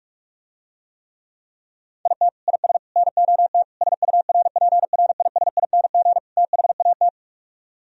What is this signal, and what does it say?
Morse code, heard as a keyed tone, about 49 wpm, 700 Hz. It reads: IT IS NOT SURPRISING THAT